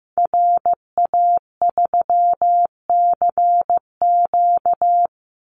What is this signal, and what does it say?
Morse code, heard as a keyed tone, about 15 wpm, 700 Hz. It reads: RA3CQ
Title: Call Signs Repeated 3x - 15wpm